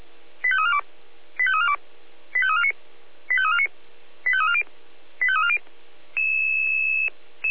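Five-tone selective call (ZVEI), heard as quick runs of five tones, tones of about 70 ms. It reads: 85342 85342 85349 85349 85340 85340
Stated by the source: Five-tone ZVEI selective call 85888, received by SDR.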